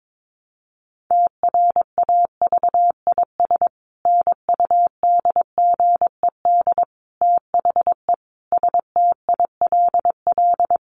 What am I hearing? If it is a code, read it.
Morse code, heard as a keyed tone, about 22 wpm, 700 Hz. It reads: TRAVIS NUDGED THE STILL
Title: A Sound Of Thunder 22wpm